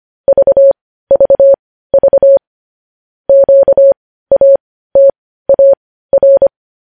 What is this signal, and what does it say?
Morse code, heard as a keyed tone, about 25 wpm, spaced as for 15 wpm, 550 Hz. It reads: VVV QATAR
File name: podcast / News headlines-1x-25x15